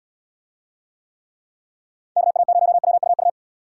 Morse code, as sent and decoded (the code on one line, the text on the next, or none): -.- .- ----- .-- ..- ..-
KA0WUU